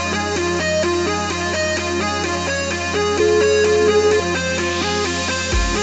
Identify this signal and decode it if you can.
Ringback tone (Europe, 425 Hz), with noise.